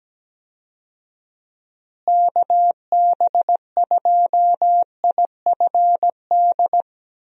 Morse code, heard as a keyed tone, about 17 wpm, 700 Hz. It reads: KB2IFD
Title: USA Call Signs - 6 Characters - Repeated 2x - 17wpm